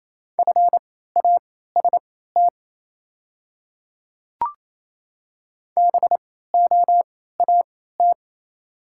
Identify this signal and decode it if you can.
Morse code, heard as a keyed tone, about 28 wpm, spaced as for 16 wpm, 700 Hz. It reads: FAST BOAT
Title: 3x Character Spacing - Top 500 Words - 28wpm